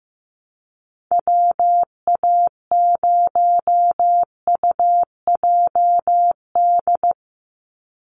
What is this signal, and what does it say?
Morse code, heard as a keyed tone, about 15 wpm, 700 Hz. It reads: WA0UJD
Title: USA Call Signs 15wpm